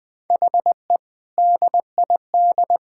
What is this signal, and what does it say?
Morse code, heard as a keyed tone, about 20 wpm, 700 Hz. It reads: HE DID
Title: Build A Phrase - 3-Word Phrases - 20wpm